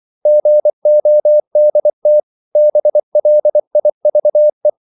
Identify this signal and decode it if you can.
Morse code, heard as a keyed tone, about 24 wpm, 600 Hz. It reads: GODT BLIVE
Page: Ferie 24 wpm